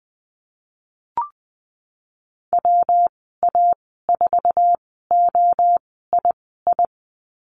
Morse code, heard as a keyed tone, about 20 wpm, spaced as for 15 wpm, 700 Hz. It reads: WA4OII
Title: 122 Lesson 20g - Number 3 - Call Signs 20wpm